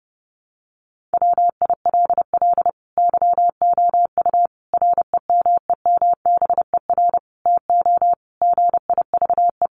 Morse code, heard as a keyed tone, about 30 wpm, 700 Hz. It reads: WILL YOU REMEMBER TO GIVE
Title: Sentences from Top 1100 Words 30wpm